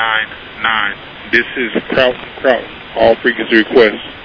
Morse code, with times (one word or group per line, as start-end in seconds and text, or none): none